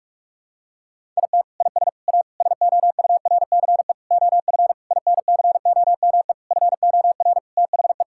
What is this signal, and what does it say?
Morse code, heard as a keyed tone, about 46 wpm, 700 Hz. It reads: IT IS A SOURCE OF INCOME FOR THE
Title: Sentences from Top 1000 Words - Review Part 5 - 45wpm